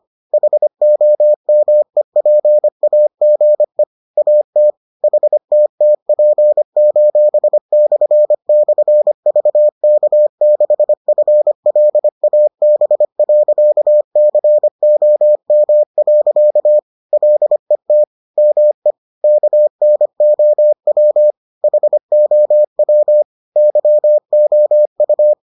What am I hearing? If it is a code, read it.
Morse code, heard as a keyed tone, about 25 wpm, 600 Hz. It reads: HOMEPAGE AT HTTP://VK6FLAB.COM. LET ME KNOW HOW YOU